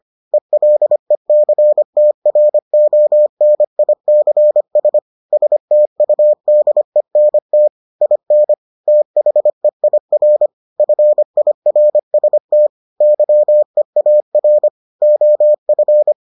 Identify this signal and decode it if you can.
Morse code, heard as a keyed tone, about 25 wpm, 600 Hz. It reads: ELECTRONICS STUDENT IN THEIR FIRST YEAR OF